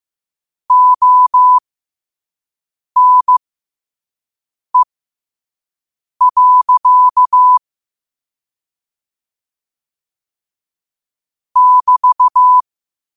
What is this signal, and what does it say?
Morse code, heard as a keyed tone, about 15 wpm, spaced as for 5 wpm, 1000 Hz. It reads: ONE. <BT>
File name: 5wpm